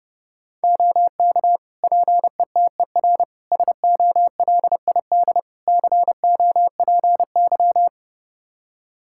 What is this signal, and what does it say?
Morse code, heard as a keyed tone, about 30 wpm, 700 Hz. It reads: OK PETER SOLID COPY
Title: K1USN SST QSO Practice 30wpm